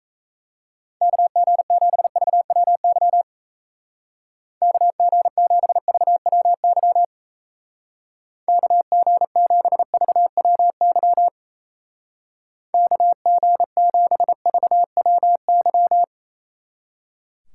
Morse code, heard as a keyed tone, about 36 wpm, 700 Hz. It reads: KG7VWY KG7VWY KG7VWY KG7VWY